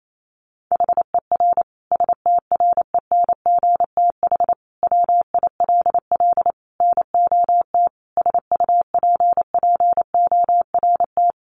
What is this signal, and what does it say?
Morse code, heard as a keyed tone, about 28 wpm, 700 Hz. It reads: HER STRENGTH WILL NOT SUPPORT
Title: Sentences from Top 1300 Words 28wpm